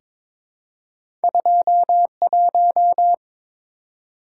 Morse code, heard as a keyed tone, about 22 wpm, 700 Hz. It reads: 21
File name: practice / 22wpm / Two Letter-Number Combinations - 22wpm